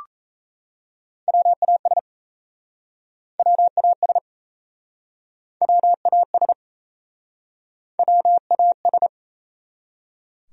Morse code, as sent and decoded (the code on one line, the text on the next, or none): .-- .- ... / .-- .- ... / .-- .- ... / .-- .- ...
WAS WAS WAS WAS